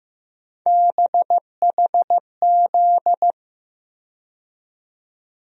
Morse code, heard as a keyed tone, about 15 wpm, 700 Hz. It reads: BHZ
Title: USA Call Sign Postfixes - 3 Letters 15wpm